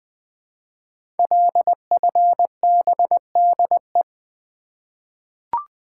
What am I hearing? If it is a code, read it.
Morse code, heard as a keyed tone, about 20 wpm, 700 Hz. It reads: LFBDE
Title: Code Groups - 5 Letters - 20wpm